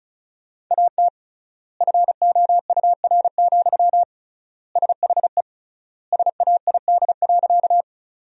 Morse code, heard as a keyed tone, about 36 wpm, 700 Hz. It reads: AT FOUR, SHE SAID.
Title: 3x Word-Spacing - Sentences from Top 300 Words - 35wpm